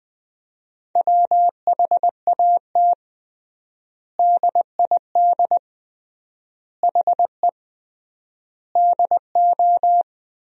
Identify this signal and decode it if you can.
Morse code, heard as a keyed tone, about 20 wpm, 700 Hz. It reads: WHAT DID HE DO
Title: 3x Word-Spacing - Sentences from Top 100 Words - 20wpm